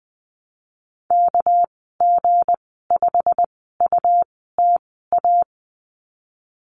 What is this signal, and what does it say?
Morse code, heard as a keyed tone, about 19 wpm, spaced as for 14 wpm, 700 Hz. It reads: KG5UTA
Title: 213 Lesson 34g - Number 8 - Call Signs 20wpm